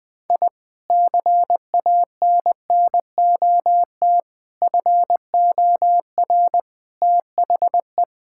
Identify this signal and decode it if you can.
Morse code, heard as a keyed tone, about 21 wpm, 700 Hz. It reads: I CANNOT FOR THE
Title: Sentences with 8-Word Phrases - 20wpm